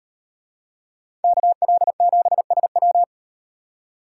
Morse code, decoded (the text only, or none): KL7SW